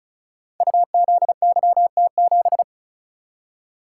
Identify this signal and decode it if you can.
Morse code, heard as a keyed tone, about 35 wpm, 700 Hz. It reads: UZYT7
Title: Code Groups - 5 Letter:Numbers - 35wpm